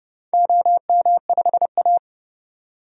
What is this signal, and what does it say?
Morse code, heard as a keyed tone, about 30 wpm, 700 Hz. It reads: OM5A
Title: DXCC-100 Call Signs - 4 Characters - 30wpm